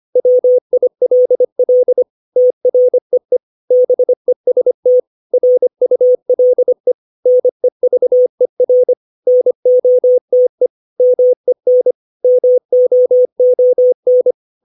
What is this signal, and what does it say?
Morse code, heard as a keyed tone, about 25 wpm, 500 Hz. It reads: WILL TREE BEST RULE NEVER NOTE MEN MOON
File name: 632 words / 632 Words 25WPM